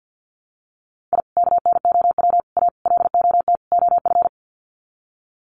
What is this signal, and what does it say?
Morse code, heard as a keyed tone, about 50 wpm, 700 Hz. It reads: I KNOW A LOT OF
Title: Sets of 5 Words Encore 50wpm